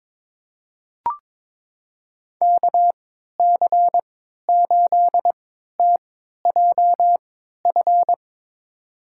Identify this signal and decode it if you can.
Morse code, heard as a keyed tone, about 22 wpm, spaced as for 12 wpm, 700 Hz. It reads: KC8TJF
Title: USA Call Signs - 6 Characters - 3x Character Spacing - 22wpm